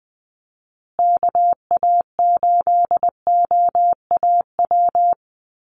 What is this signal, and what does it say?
Morse code, heard as a keyed tone, about 20 wpm, 700 Hz. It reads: KA8OAW
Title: USA Call Signs Repeated 3x 20wpm